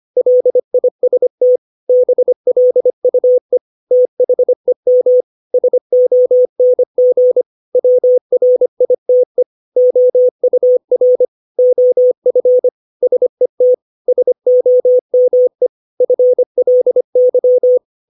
Morse code, heard as a keyed tone, about 25 wpm, 500 Hz. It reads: LIST BLUE THEM SONG WRITE OUR OF SET SOME FLY
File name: 632 words / 632 Words 25WPM